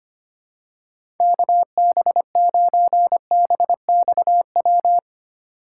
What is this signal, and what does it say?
Morse code, heard as a keyed tone, about 25 wpm, 700 Hz. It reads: KB9BXW
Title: USA Call Signs - 6 Characters - 3x Character Spacing - 25wpm